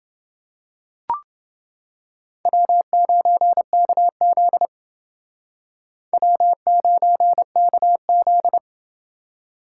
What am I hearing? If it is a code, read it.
Morse code, heard as a keyed tone, about 29 wpm, 700 Hz. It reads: W9KZ W9KZ